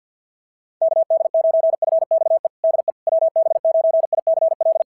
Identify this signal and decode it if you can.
Morse code, heard as a keyed tone, about 52 wpm, 690 Hz. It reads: KD9FXE DE WB9ICL
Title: SOTA QSO Practice 50wpm